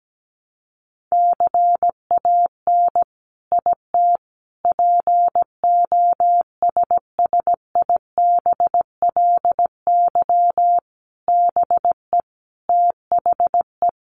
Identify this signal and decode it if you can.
Morse code, heard as a keyed tone, about 17 wpm, 700 Hz. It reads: CAN IT POSSIBLY BE THE
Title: Sentences from Top 1300 Words 17wpm